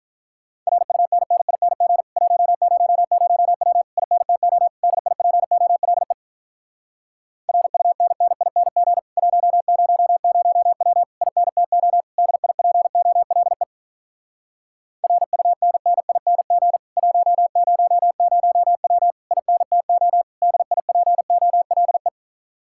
Morse code, heard as a keyed tone, about 47 wpm, 700 Hz. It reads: RUNNING 100W INTO DIPOLE RUNNING 100W INTO DIPOLE RUNNING 100W INTO DIPOLE